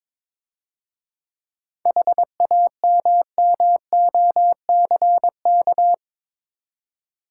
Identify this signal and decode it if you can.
Morse code, heard as a keyed tone, about 22 wpm, 690 Hz. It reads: HAMMOCK
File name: Easy Words - Speed-Racing with Hints - 22wpm